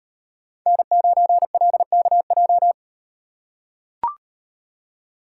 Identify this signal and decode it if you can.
Morse code, heard as a keyed tone, about 36 wpm, 700 Hz. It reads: N9LKJ